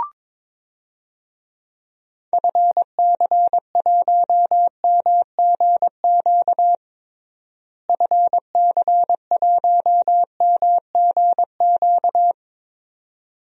Morse code, heard as a keyed tone, about 22 wpm, 700 Hz. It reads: FC1MGQ FC1MGQ
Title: DXCC-100 Call Signs - 6 Characters - Repeated 2x - 22wpm